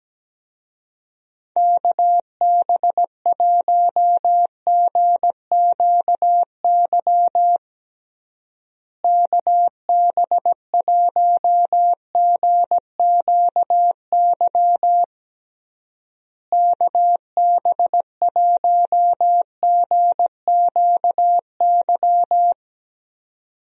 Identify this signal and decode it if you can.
Morse code, heard as a keyed tone, about 17 wpm, 700 Hz. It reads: KB1GQY KB1GQY KB1GQY
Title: USA Call Signs - 6 Characters - Repeated 3x - 17wpm